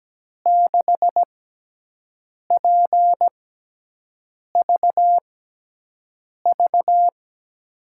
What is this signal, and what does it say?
Morse code, heard as a keyed tone, about 17 wpm, spaced as for 6 wpm, 700 Hz. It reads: 6PVV
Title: USA Call Signs - 6 Characters - 6x Character Spacing - 17wpm